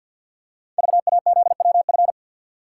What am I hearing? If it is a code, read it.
Morse code, heard as a keyed tone, about 53 wpm, 700 Hz. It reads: VAZWF
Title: Code Groups - 5 Letter:Numbers - 50wpm